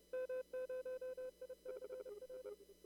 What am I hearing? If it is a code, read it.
Morse code, heard as a keyed tone, about 31 wpm, 510 Hz. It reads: M0ICR
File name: CW / kt5j wpx cw-op-k5pi